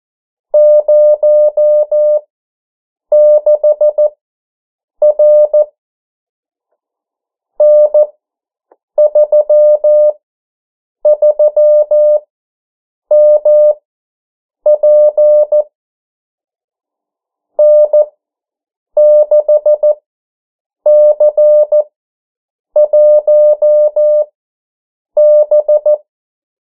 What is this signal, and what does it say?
Morse code, heard as a keyed tone, about 14 wpm, spaced as for 7 wpm, 600 Hz. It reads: 06R N33MP N6C1B
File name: morse mp3 / 09 urok 8